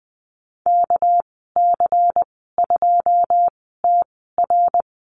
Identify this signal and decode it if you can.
Morse code, heard as a keyed tone, about 20 wpm, spaced as for 15 wpm, 700 Hz. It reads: KC2TR
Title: 213 Lesson 34g - Number 8 - Call Signs 20wpm